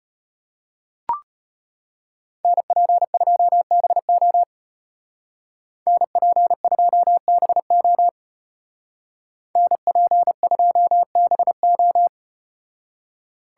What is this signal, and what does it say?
Morse code, heard as a keyed tone, about 34 wpm, 700 Hz. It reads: NP2BO NP2BO NP2BO